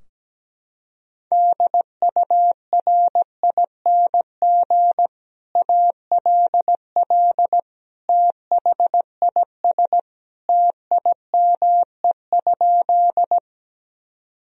Morse code, heard as a keyed tone, about 17 wpm, 700 Hz. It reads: DURING ALL THIS TIME?
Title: Sentences from Top 1000 Words - Review Part 2 - 17wpm